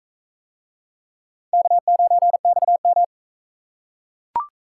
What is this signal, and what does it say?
Morse code, heard as a keyed tone, about 38 wpm, 700 Hz. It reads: K9XM